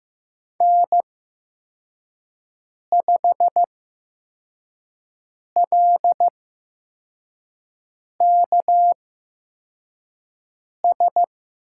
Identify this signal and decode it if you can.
Morse code, heard as a keyed tone, about 15 wpm, spaced as for 4 wpm, 700 Hz. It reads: N5LKS
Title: USA Call Signs - 5 Characters - 8x Character Spacing - 15wpm